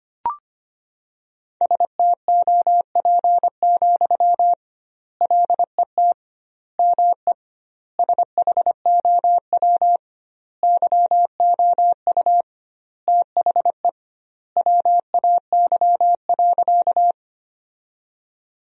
Morse code, heard as a keyed tone, about 25 wpm, 700 Hz. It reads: STOP, LET ME SHOW YOU THE WAY.
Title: 2x Word-Spacing - Sentences from Top 300 Words - 25wpm